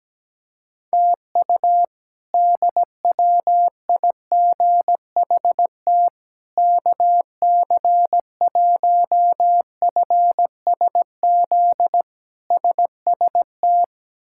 Morse code, heard as a keyed tone, about 17 wpm, 700 Hz. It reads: TU DWIGHT KC1FSZ SST
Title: K1USN SST QSO Practice - 17wpm